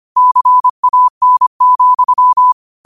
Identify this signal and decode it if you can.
Morse code, heard as a keyed tone, about 25 wpm, 1000 Hz. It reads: CAN,
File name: qotdfiles / 25wpm